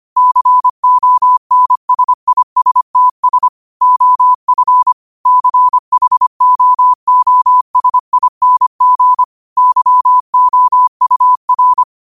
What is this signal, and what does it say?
Morse code, heard as a keyed tone, about 25 wpm, 1000 Hz. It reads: CONSISTS OF CHOOSING YOUR